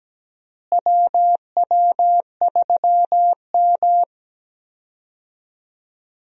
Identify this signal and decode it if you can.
Morse code, heard as a keyed tone, about 17 wpm, 700 Hz. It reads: WW3M
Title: USA Call Signs - 4 Characters - Repeated 2x - 17wpm